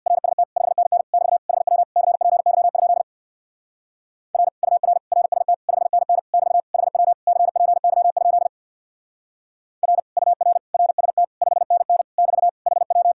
Morse code, heard as a keyed tone, about 58 wpm, 700 Hz. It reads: RST 5NN <BT> HW CPY? R UR RST 5NN <BT> HW CPY? R UR RST 5NN <BT> HW
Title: Speed-Racing Common QSO Phrases - 50wpm